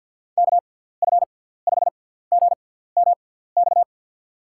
Morse code, heard as a keyed tone, about 47 wpm, spaced as for 17 wpm, 700 Hz. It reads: KF5GMX
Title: USA Call Signs - 6x Character Spacing - 50wpm